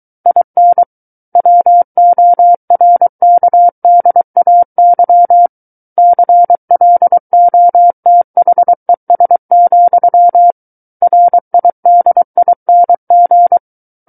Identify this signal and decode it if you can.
Morse code, heard as a keyed tone, about 23 wpm, 700 Hz. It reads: IN WORKDAY CLOTHES, RIDING
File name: Book One / 14-Chapter 14 - 23 WPM